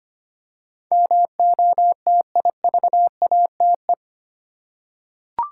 Morse code, heard as a keyed tone, about 25 wpm, 700 Hz. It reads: MOTIVATE
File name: 8 Letter Words - First 3 Letters Given - 25wpm